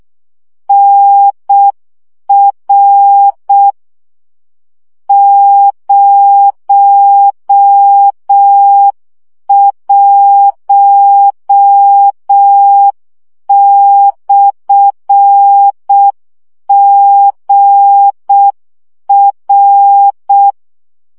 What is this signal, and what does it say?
Morse code, heard as a keyed tone, about 6 wpm, 800 Hz. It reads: NR 01/GR
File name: morze vizsga 2011-1 5